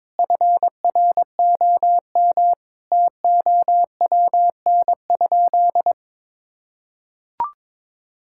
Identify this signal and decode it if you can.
Morse code, heard as a keyed tone, about 21 wpm, 700 Hz. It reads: FROM TOWN?
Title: Sentences from Top 1000 Words - Review Part 5 - 22wpm